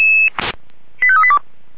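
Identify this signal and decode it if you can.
Five-tone selective call (ZVEI), heard as quick runs of five tones, tones of about 70 ms. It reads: 85382
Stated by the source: Five-tone ZVEI selective call received on an SDR.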